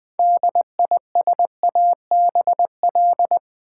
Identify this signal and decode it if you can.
Morse code, heard as a keyed tone, about 20 wpm, 700 Hz. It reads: DISABL
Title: Build A Word - 3 Letters At A Time - 8-Letter Words - 20wpm